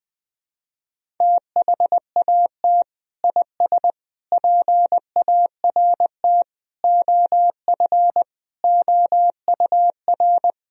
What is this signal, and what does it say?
Morse code, heard as a keyed tone, about 20 wpm, 700 Hz. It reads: THAT IS PART OF OUR
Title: Sentences from Top 1000 Words - Review Part 1 - 20wpm